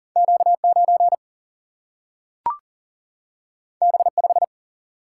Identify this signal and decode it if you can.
Morse code, heard as a keyed tone, about 40 wpm, 700 Hz. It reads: Q 9 B 5
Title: Single-Letter Single-Number 40wpm